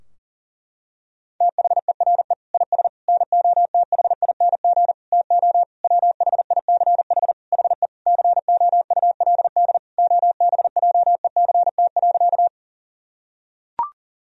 Morse code, heard as a keyed tone, about 37 wpm, 700 Hz. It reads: THERE IS NOTHING TO WHICH HE COULD OBJECT.
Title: Sentences from Top 1000 Words - Review Part 3 - 40wpm